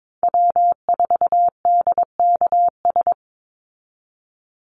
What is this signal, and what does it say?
Morse code, heard as a keyed tone, about 22 wpm, 700 Hz. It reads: W4DKS
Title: Call Signs Repeated 3x - 22wpm